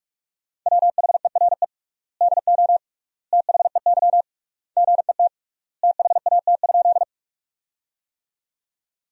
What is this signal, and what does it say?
Morse code, heard as a keyed tone, about 46 wpm, 700 Hz. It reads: WHERE DO THEY GET THAT?
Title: 3x Word-Spacing - Sentences from Top 200 Words - 45wpm